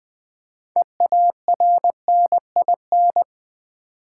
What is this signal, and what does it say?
Morse code, heard as a keyed tone, about 20 wpm, 700 Hz. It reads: EARNIN
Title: Build A Word - 3 Letters At A Time - 8-Letter Words - 20wpm